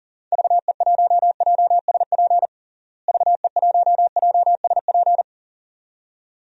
Morse code, heard as a keyed tone, about 40 wpm, 700 Hz. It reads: VE1JSP VE1JSP
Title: DXCC-100 Call Signs - 6 Characters - Repeated 2x - 40wpm